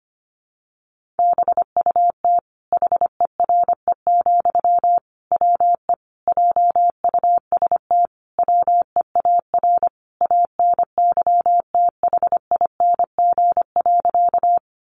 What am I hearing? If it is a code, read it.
Morse code, heard as a keyed tone, about 25 wpm, 700 Hz. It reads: BUT HERE, WE JUST WEAR ANYTHING.
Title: Sentences from Top 1500 Words 25wpm